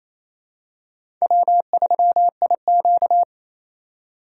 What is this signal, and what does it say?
Morse code, heard as a keyed tone, about 28 wpm, 700 Hz. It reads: W3IQ
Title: USA Call Signs - 4 Characters - 3x Character Spacing - 28wpm